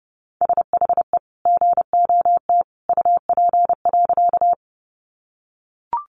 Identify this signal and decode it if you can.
Morse code, heard as a keyed tone, about 30 wpm, 700 Hz. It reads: SHE GOT UP.
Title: Sentences from Top 300 Words 30wpm